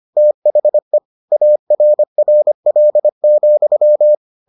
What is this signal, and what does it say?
Morse code, heard as a keyed tone, about 25 wpm, 600 Hz. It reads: THE ARRL,